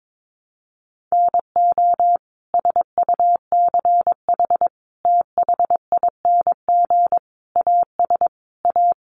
Morse code, heard as a keyed tone, about 22 wpm, 700 Hz. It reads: NO SUCH THING AS A
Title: Sets of 5 Words Encore 22wpm